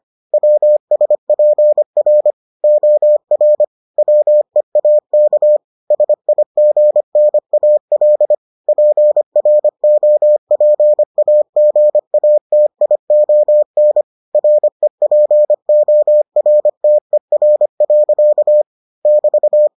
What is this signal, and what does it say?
Morse code, heard as a keyed tone, about 25 wpm, 600 Hz. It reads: WSPR OR WEAK SIGNAL PROPAGATION REPORTER. <BT>